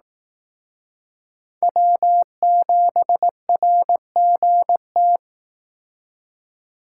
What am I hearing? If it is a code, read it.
Morse code, heard as a keyed tone, about 18 wpm, 700 Hz. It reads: W7RGT
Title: Speed Racing - USA 1x3 Call Signs 15wpm